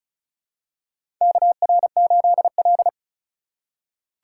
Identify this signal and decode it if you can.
Morse code, heard as a keyed tone, about 36 wpm, 700 Hz. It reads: KR8L
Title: K1USN SST QSO Practice - 35wpm